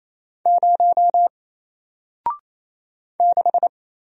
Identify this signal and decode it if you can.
Morse code, heard as a keyed tone, about 27 wpm, 700 Hz. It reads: 0 6
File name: Single-Letter-Number-28wpm